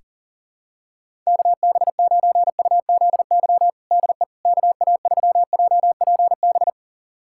Morse code, heard as a keyed tone, about 40 wpm, 700 Hz. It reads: KB9UZY DE KA3JPB
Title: USA Callsign DE Callsign - 6 Characters - 40wpm